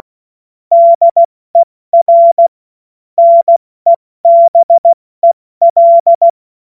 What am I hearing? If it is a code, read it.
Morse code, heard as a keyed tone, about 16 wpm, 680 Hz. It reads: DER NEBEL
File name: Morsekurs / Witze / Witz 16. August 70 F80